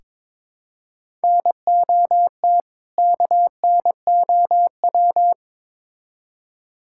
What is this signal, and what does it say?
Morse code, heard as a keyed tone, about 22 wpm, 700 Hz. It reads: NOT KNOW
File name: Build A Phrase - 2-Word Phrases - 22wpm